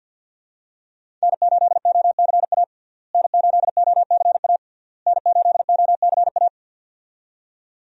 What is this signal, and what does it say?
Morse code, heard as a keyed tone, about 51 wpm, 700 Hz. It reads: N8OCA N8OCA N8OCA
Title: USA Call Signs - 5 Characters - Repeated 3x - 50wpm